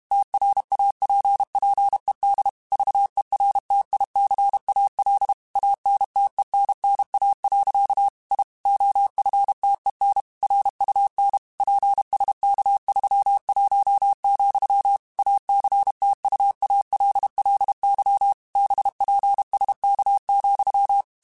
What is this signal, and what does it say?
Morse code, heard as a keyed tone, about 32 wpm, 800 Hz. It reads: TRAPPED VERTICAL ANTENNA. I OFTEN RUN PSK31, ACTUALLY BPSK,